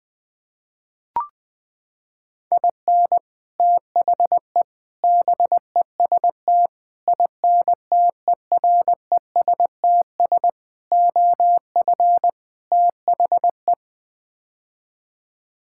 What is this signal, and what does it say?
Morse code, heard as a keyed tone, about 20 wpm, 700 Hz. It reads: IN THE BEST INTERESTS OF THE
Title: Sets of 6 Words - 20wpm